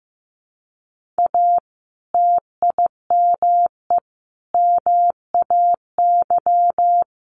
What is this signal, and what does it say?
Morse code, heard as a keyed tone, about 15 wpm, 700 Hz. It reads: A TIME MAY
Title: Sentences from Top 1500 Words 15wpm